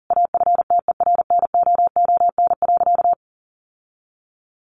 Morse code, heard as a keyed tone, about 40 wpm, 700 Hz. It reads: AFTERNOON.
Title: Sentences from Top 1500 Words 40wpm